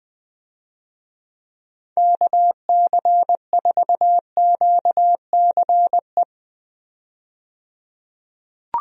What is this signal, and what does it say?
Morse code, heard as a keyed tone, about 20 wpm, 700 Hz. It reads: KC4QCE
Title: USA Call Signs - 6 Characters - 20wpm